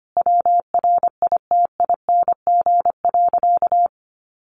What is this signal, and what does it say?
Morse code, heard as a keyed tone, about 25 wpm, 700 Hz. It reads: WRITING.
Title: Sentences from Top 800 Words 25wpm